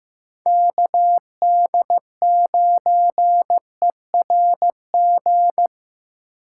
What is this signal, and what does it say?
Morse code, heard as a keyed tone, about 15 wpm, 700 Hz. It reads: KD9ERG
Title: USA Call Signs - 6 Characters - 4x Character Spacing - 15wpm